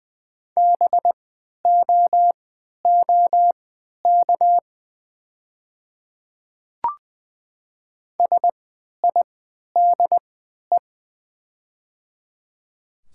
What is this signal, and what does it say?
Morse code, heard as a keyed tone, about 20 wpm, spaced as for 11 wpm, 700 Hz. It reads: BOOK SIDE
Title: 3x Character Spacing - Top 300 Words - 20wpm